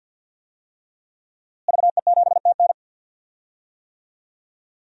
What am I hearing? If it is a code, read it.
Morse code, heard as a keyed tone, about 52 wpm, 700 Hz. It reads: VE7TN
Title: DXCC-100 Call Signs - 5 Characters - Repeated 3x - 50wpm